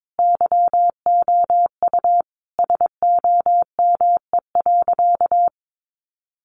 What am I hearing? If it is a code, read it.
Morse code, heard as a keyed tone, about 22 wpm, 700 Hz. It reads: YOU SOME.
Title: Sentences from Top 800 Words 22wpm